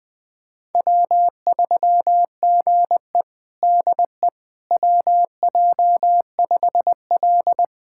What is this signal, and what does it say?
Morse code, heard as a keyed tone, about 20 wpm, 700 Hz. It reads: W3GE DE WJ5L
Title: USA Callsign DE Callsign - 4 Characters - 20wpm